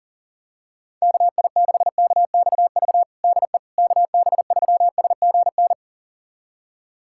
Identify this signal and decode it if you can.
Morse code, heard as a keyed tone, about 40 wpm, 700 Hz. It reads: KI6KXV DE KB3SGN
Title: USA Callsign DE Callsign 40wpm